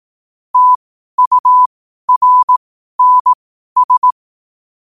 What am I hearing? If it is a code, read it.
Morse code, heard as a keyed tone, about 18 wpm, spaced as for 12 wpm, 1000 Hz. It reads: TURNS
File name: qotdfiles / 12wpm